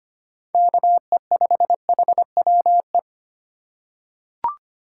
Morse code, heard as a keyed tone, about 25 wpm, 700 Hz. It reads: KE5HWE
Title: USA Call Signs - 6 Characters - 3x Character Spacing - 25wpm